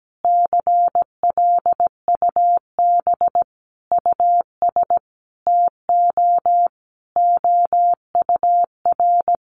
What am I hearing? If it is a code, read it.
Morse code, heard as a keyed tone, about 17 wpm, 700 Hz. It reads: CLUB US TO OUR
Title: Sentences from Top 1200 Words 17wpm